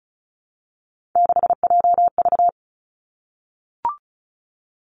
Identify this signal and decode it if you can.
Morse code, heard as a keyed tone, about 36 wpm, 700 Hz. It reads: 6JV
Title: Three Challenging Letter-Number Combinations 35wpm